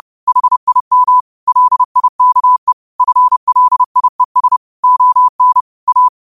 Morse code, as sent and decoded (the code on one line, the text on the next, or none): .... .. -- / .-.. .. -.- . / ..-. .-.. .. ... / --- -. / .-
HIM LIKE FLIES ON A